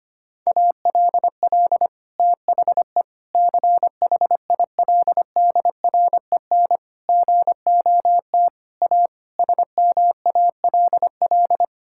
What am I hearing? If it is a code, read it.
Morse code, heard as a keyed tone, about 25 wpm, 700 Hz. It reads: ALL THE CHILDREN GOT A SMALL
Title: Sentences from Top 1000 Words - Review Part 6 - 25wpm